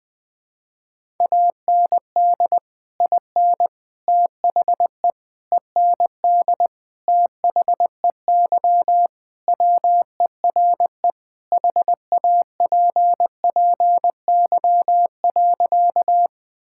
Morse code, heard as a keyed tone, about 21 wpm, 700 Hz. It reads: AND IN THE END THEY WERE HAPPY.
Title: Sentences with 2-Word Phrases - 20wpm